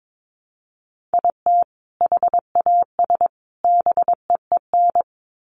Morse code, heard as a keyed tone, about 22 wpm, 700 Hz. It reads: IT HAS BEEN
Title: Sets of 3 Words 22wpm